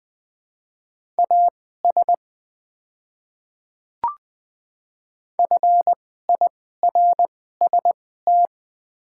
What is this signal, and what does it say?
Morse code, heard as a keyed tone, about 20 wpm, spaced as for 15 wpm, 700 Hz. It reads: AS FIRST
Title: 2x Character Spacing - Top 100 Words - 20wpm